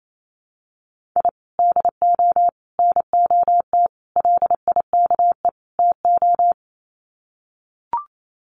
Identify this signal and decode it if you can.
Morse code, heard as a keyed tone, about 28 wpm, 700 Hz. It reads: I DO NOT LIKE TO
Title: Sets of 5 Words 28wpm